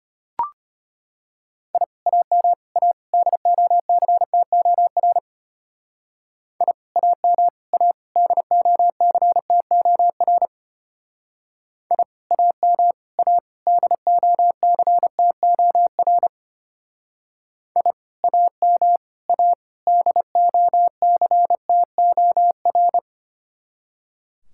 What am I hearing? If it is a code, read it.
Morse code, heard as a keyed tone, about 31 wpm, 700 Hz. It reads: I AM A DOCTOR I AM A DOCTOR I AM A DOCTOR I AM A DOCTOR
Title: Speed-Racing Common QSO Phrases Encore - 25wpm